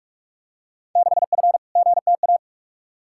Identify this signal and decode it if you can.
Morse code, heard as a keyed tone, about 46 wpm, 700 Hz. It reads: 6F GTA
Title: Field Day Exchanges Encore - 45wpm